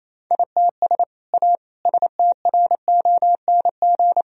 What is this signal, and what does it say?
Morse code, heard as a keyed tone, about 28 wpm, 700 Hz. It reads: ITS A STRONG
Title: Sentences from Top 1000 Words - Review Part 4 - 28wpm